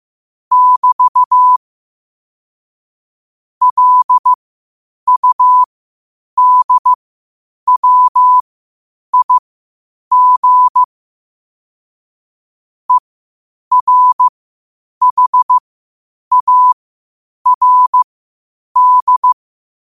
Morse code, heard as a keyed tone, about 15 wpm, spaced as for 8 wpm, 1000 Hz. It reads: <BT> LUDWIG ERHARD
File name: qotdfiles / 8wpm